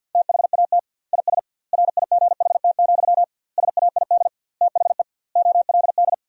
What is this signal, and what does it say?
Morse code, heard as a keyed tone, about 53 wpm, 700 Hz. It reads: THAT IS RIGHT, SAID THE OLD
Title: 2x Word-Spacing - Sentences from Top 200 Words - 50wpm